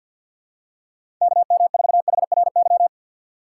Morse code, heard as a keyed tone, about 52 wpm, 700 Hz. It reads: KM4HRY
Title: USA Call Signs - 6 Characters - 4x Character Spacing - 50wpm